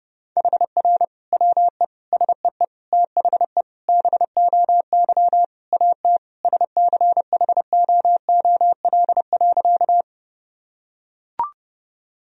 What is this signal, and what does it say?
Morse code, heard as a keyed tone, about 30 wpm, 700 Hz. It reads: HR WE SEE THE BOY AT SCHOOL.
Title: Sentences from Top 300 Words and Abbreviations 30wpm